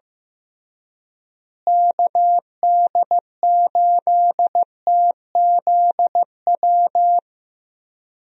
Morse code, heard as a keyed tone, about 15 wpm, 700 Hz. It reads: KD8TZW
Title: USA Call Signs - 6 Characters - Repeated 3x - 15wpm